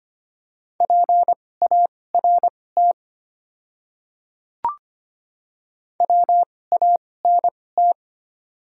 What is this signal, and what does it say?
Morse code, heard as a keyed tone, about 25 wpm, spaced as for 18 wpm, 700 Hz. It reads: PART WANT
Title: 2x Character Spacing - Top 200 Words - 25wpm